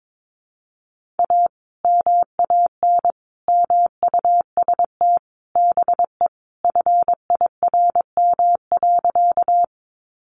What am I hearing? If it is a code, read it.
Morse code, heard as a keyed tone, about 22 wpm, 700 Hz. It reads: A MAN MUST BE FIRM.
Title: Sentences from Top 1400 Words 22wpm